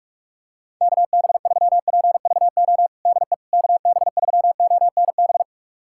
Morse code, heard as a keyed tone, about 41 wpm, 700 Hz. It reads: KB3PVO DE KB3ONB
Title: USA Callsign DE Callsign - 6 Characters - 45wpm